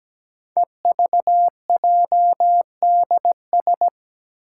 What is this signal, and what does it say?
Morse code, heard as a keyed tone, about 17 wpm, 700 Hz. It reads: EVJDS
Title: Code Groups - 5 Letters - 17wpm